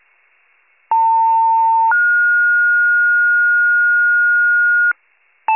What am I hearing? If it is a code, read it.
Two-tone page, heard as a one-second tone, then a three-second tone, about 900 then 1430 Hz, with noise. It is heard from 0.9 to 4.9 s.
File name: pager / 78 FIRE / wcalert